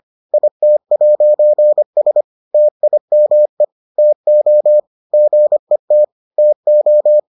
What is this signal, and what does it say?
Morse code, heard as a keyed tone, about 25 wpm, 600 Hz. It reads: IT'S TIME TO GET TO